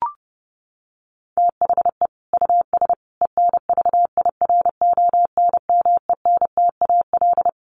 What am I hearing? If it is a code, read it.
Morse code, heard as a keyed tone, about 30 wpm, 700 Hz. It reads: THE US ENVIRONMENTAL
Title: Sets of 5 Words Encore2 30wpm